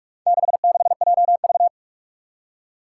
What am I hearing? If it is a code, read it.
Morse code, heard as a keyed tone, about 46 wpm, 700 Hz. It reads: 66JV
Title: Four Challenging Letter-Number Combinations 45wpm